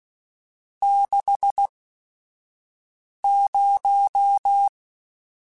Morse code, heard as a keyed tone, about 16 wpm, spaced as for 5 wpm, 800 Hz. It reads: 60
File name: Weather Sets / Weather Sets-mp3s / Weather 5 22 @5